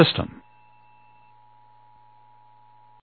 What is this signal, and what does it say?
Emergency Alert System (SAME) tones (the 853 and 960 Hz attention tone), with speech. It is heard from 0.4 to 3.0 s.
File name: eas / KWBE / KWBE